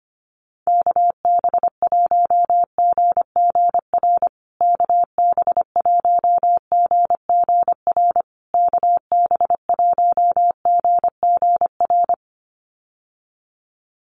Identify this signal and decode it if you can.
Morse code, heard as a keyed tone, about 25 wpm, 700 Hz. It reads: KB1GGR KB1GGR KB1GGR
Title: USA Call Signs Repeated 3x 25wpm